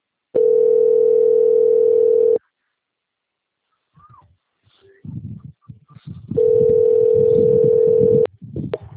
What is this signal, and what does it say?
Ringback tone (North America), 440 + 480 Hz.
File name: out- - - - - .29749